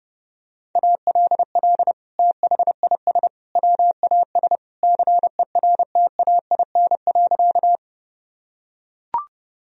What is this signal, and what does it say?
Morse code, heard as a keyed tone, about 30 wpm, 700 Hz. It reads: ALL THIS WAS CERTAIN.
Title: Sentences from Top 1000 Words - Review Part 2 - 30wpm